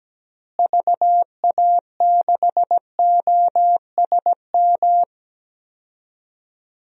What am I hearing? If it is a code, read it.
Morse code, heard as a keyed tone, about 17 wpm, 700 Hz. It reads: VA6OSM
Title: DXCC-100 Call Signs - 6 Characters - Repeated 2x - 17wpm